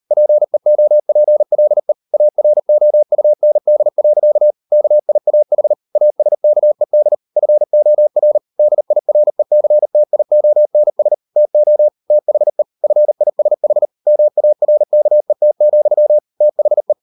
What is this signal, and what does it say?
Morse code, heard as a keyed tone, about 38 wpm, 600 Hz. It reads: PEOPLE AROUND. KIAH ASKED FOR DIRECTIONS TO THE FISH MARKET, THE